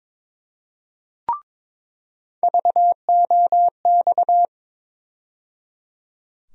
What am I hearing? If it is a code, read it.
Morse code, heard as a keyed tone, about 22 wpm, 700 Hz. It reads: VOX
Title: USA Call Sign Postfixes - 3 Letters 22wpm